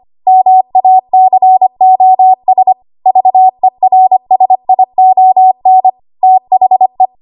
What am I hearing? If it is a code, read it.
Morse code, heard as a keyed tone, about 25 wpm, 750 Hz. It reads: MACOS VERSION THE
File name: Archive / 25 WPM / 25WPM